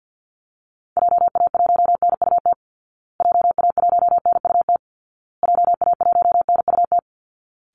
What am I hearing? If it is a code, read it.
Morse code, heard as a keyed tone, about 50 wpm, 700 Hz. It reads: JA1NUT JA1NUT JA1NUT
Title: Call Signs Repeated 2x - 50wpm